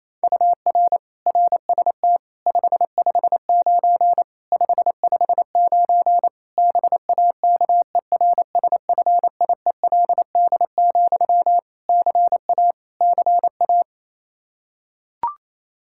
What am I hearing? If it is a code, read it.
Morse code, heard as a keyed tone, about 28 wpm, 700 Hz. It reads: UR RST 559 559 BAKERSFIELD, CA CA
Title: K1USN SST QSO Practice 28wpm